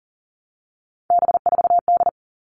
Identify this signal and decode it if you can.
Morse code, heard as a keyed tone, about 40 wpm, 700 Hz. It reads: B4D
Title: All Cut Numbers - 3-Digit 40wpm